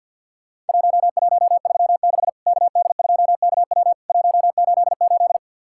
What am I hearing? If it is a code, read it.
Morse code, heard as a keyed tone, about 50 wpm, 700 Hz. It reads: 1136 KD2KW 188
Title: Worked All Europe DX Contest - QTC Exchange with Repeats - 50wpm